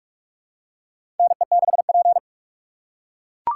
Morse code, heard as a keyed tone, about 44 wpm, 700 Hz. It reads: NE6P
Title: USA Call Signs - 4 Characters - 6x Character Spacing - 45wpm